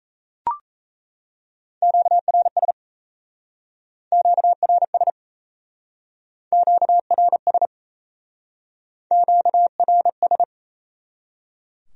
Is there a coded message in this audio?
Yes: Morse code from 1.8 to 10.4 s, heard as a keyed tone, about 37 wpm, 700 Hz. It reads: QRS QRS QRS QRS